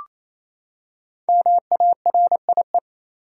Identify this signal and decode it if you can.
Morse code, heard as a keyed tone, about 28 wpm, 700 Hz. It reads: MARIE
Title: USA Common Names - 28wpm